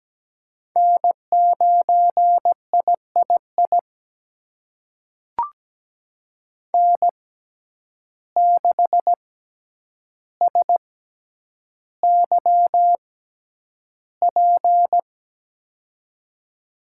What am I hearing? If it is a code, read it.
Morse code, heard as a keyed tone, about 17 wpm, spaced as for 9 wpm, 700 Hz. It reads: N9III N6SYP
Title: USA Call Signs - 5 Characters - 6x Character Spacing - 17wpm